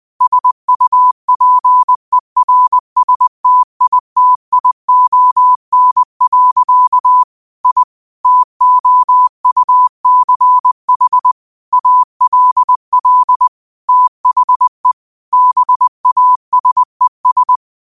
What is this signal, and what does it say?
Morse code, heard as a keyed tone, about 20 wpm, 1000 Hz. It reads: SUPERSTITION. I TOUCH ALL THE BASES
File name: qotdfiles / 20wpm